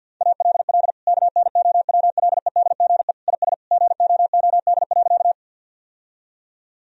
Morse code, heard as a keyed tone, about 50 wpm, 700 Hz. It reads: ALL KNOWLEDGE IS GOOD.
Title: Sentences from Top 1000 Words - Review Part 3 - 50wpm